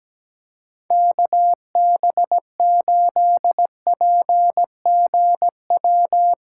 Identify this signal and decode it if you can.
Morse code, heard as a keyed tone, about 17 wpm, 700 Hz. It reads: KB8PGW
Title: K1USN SST QSO Practice - 17wpm